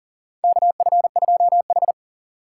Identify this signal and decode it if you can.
Morse code, heard as a keyed tone, about 40 wpm, 700 Hz. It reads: KF2H
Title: USA Call Signs - 6x Character Spacing - 40wpm